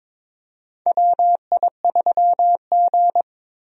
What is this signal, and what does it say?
Morse code, heard as a keyed tone, about 22 wpm, 700 Hz. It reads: WI3G
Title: USA Call Signs - 4 Characters - Repeated 3x - 22wpm